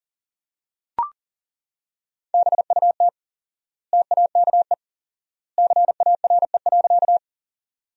Morse code, heard as a keyed tone, about 40 wpm, 700 Hz. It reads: BUT TAKE CARE.
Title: 4x Word-Spacing - Sentences from Top 300 Words - 40wpm